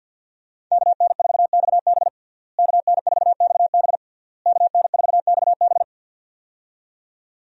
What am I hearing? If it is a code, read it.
Morse code, heard as a keyed tone, about 52 wpm, 700 Hz. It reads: KN4XB KN4XB KN4XB
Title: USA Call Signs - 5 Characters - Repeated 3x - 50wpm